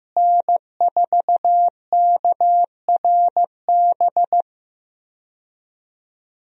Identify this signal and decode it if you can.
Morse code, heard as a keyed tone, about 16 wpm, 700 Hz. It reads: N4KRB
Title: USA Call Signs - 5 Characters - Repeated 2x - 15wpm